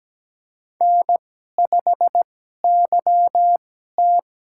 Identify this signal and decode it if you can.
Morse code, heard as a keyed tone, about 17 wpm, spaced as for 12 wpm, 700 Hz. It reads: N5YT
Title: USA Call Signs - 2x Character Spacing - 17wpm